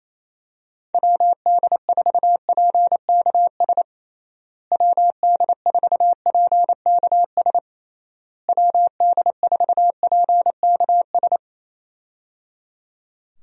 Morse code, heard as a keyed tone, about 27 wpm, 700 Hz. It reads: WD4PKS WD4PKS WD4PKS
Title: USA Call Signs - 6 Characters - Repeated 3x - 28wpm